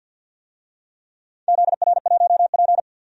Morse code, heard as a keyed tone, about 51 wpm, 700 Hz. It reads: ZR1P
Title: DXCC-100 Call Signs - 4 Characters - 50wpm